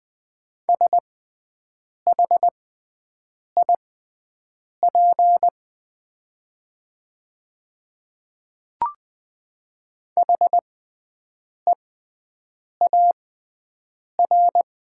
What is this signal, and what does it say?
Morse code, heard as a keyed tone, about 20 wpm, spaced as for 7 wpm, 700 Hz. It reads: SHIP HEAR
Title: 6x Character Spacing - Top 500 Words - 20wpm